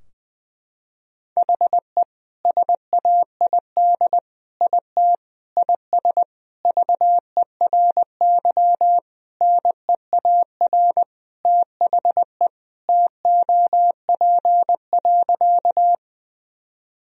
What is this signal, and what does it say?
Morse code, heard as a keyed tone, about 20 wpm, 700 Hz. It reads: HE SAID IT IS VERY NEAR THE TOP.
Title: Sentences from Top 1000 Words - Review Part 2 - 20wpm